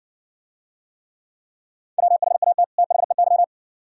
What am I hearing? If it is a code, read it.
Morse code, heard as a keyed tone, about 56 wpm, 700 Hz. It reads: WHAT THEY